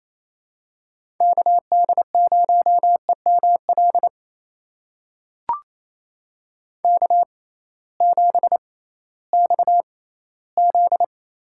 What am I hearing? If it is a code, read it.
Morse code, heard as a keyed tone, about 28 wpm, spaced as for 16 wpm, 700 Hz. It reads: KD0EML K7XZ
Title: USA Call Signs - 6x Character Spacing - 28wpm